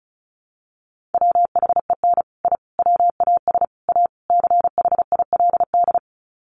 Morse code, heard as a keyed tone, about 34 wpm, 700 Hz. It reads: WHEN I WAS A CHILD
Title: Sets of 5 Words Encore 35wpm